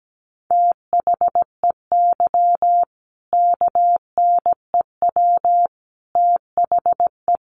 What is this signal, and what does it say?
Morse code, heard as a keyed tone, about 17 wpm, 700 Hz. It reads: THEY KNEW THE
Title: Sentences from Top 1400 Words 17wpm